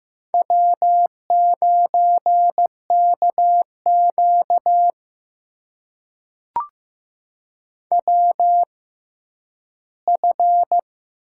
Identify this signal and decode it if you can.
Morse code, heard as a keyed tone, about 15 wpm, spaced as for 10 wpm, 700 Hz. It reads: W9KQ WF
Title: USA Call Signs - 4 Characters - 6x Character Spacing - 15wpm